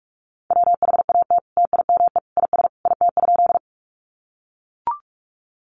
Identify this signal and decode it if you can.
Morse code, heard as a keyed tone, about 44 wpm, 700 Hz. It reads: WHAT TIME IS IT?
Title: Sentences from Top 100 Words 45wpm